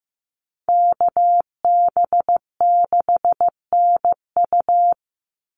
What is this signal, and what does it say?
Morse code, heard as a keyed tone, about 15 wpm, 700 Hz. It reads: KB6NU
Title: Call Signs Repeated 2x - 15wpm